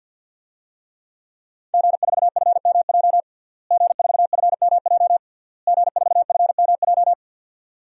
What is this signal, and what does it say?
Morse code, heard as a keyed tone, about 52 wpm, 700 Hz. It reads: G4FMJ G4FMJ G4FMJ
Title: DXCC-100 Call Signs - 5 Characters - Repeated 3x - 50wpm